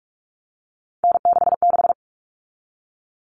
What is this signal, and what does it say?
Morse code, heard as a keyed tone, about 44 wpm, 700 Hz. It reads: N66
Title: All Cut Numbers - 3-Digit 45wpm